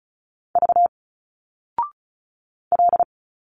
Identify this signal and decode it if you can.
Morse code, heard as a keyed tone, about 35 wpm, 700 Hz. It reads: V L
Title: Single Letters 35wpm